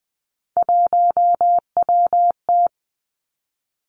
Morse code, heard as a keyed tone, about 20 wpm, 700 Hz. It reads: 1WT
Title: Three Challenging Letter-Number Combinations 20wpm